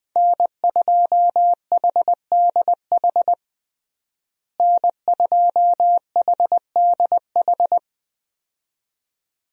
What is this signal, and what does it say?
Morse code, heard as a keyed tone, about 20 wpm, 700 Hz. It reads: N2HDH N2HDH
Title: USA Call Signs - 5 Characters - Repeated 2x - 20wpm